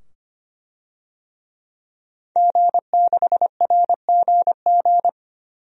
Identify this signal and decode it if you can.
Morse code, heard as a keyed tone, about 25 wpm, 700 Hz. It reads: G6RGG
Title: DXCC-100 Call Signs - 5 Characters - 25wpm